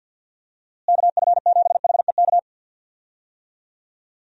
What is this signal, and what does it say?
Morse code, heard as a keyed tone, about 50 wpm, 700 Hz. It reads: KF7HEK
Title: USA Call Signs - 6 Characters - Repeated 3x - 50wpm